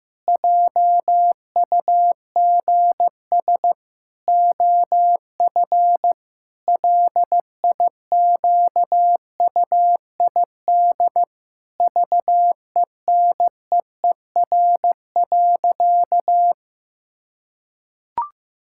Morse code, heard as a keyed tone, about 15 wpm, 700 Hz. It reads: JUGS OF LIQUID VENEER.
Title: Pangram Challenge v2 - 15wpm